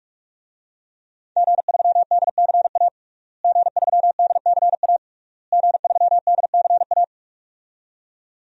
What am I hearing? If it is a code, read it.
Morse code, heard as a keyed tone, about 41 wpm, 700 Hz. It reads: G3DCA G3DCA G3DCA
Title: DXCC-100 Call Signs - 5 Characters - Repeated 3x - 45wpm